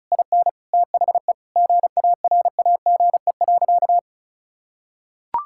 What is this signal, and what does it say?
Morse code, heard as a keyed tone, about 34 wpm, 700 Hz. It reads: IN THE GARAGE.
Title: Sentences from Top 1600 Words 35wpm